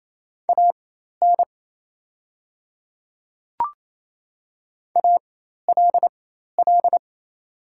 Morse code, heard as a keyed tone, about 27 wpm, spaced as for 13 wpm, 700 Hz. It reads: AN ALL
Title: 4x Character Spacing - Top 100 Words - 28wpm